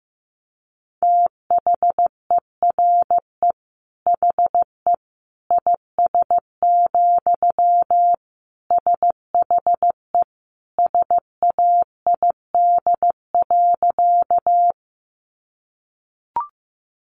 Morse code, heard as a keyed tone, about 15 wpm, 700 Hz. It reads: THERE HE IS, SHE SAID.
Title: Sentences from Top 100 Words 15wpm